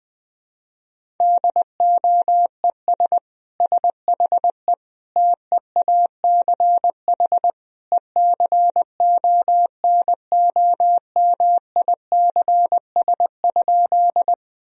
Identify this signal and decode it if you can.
Morse code, heard as a keyed tone, about 20 wpm, 700 Hz. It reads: DOES SHE TEACH ECONOMICS?
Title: Sentences from Top 1000 Words - Review Part 3 - 20wpm